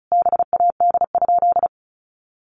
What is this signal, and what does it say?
Morse code, heard as a keyed tone, about 34 wpm, 700 Hz. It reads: BAD?
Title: Sentences from Top 800 Words 35wpm